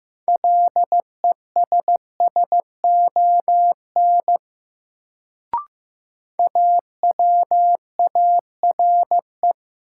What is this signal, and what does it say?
Morse code, heard as a keyed tone, about 15 wpm, 700 Hz. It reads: LESSON AWARE
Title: Sentences from Top 1200 Words - New Word Drill Alt - 15wpm